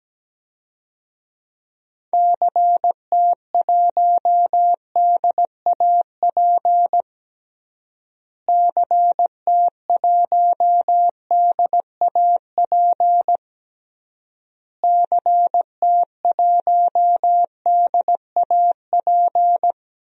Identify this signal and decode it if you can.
Morse code, heard as a keyed tone, about 17 wpm, 700 Hz. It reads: CT1DAP CT1DAP CT1DAP
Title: DXCC-100 Call Signs - 6 Characters - Repeated 3x - 17wpm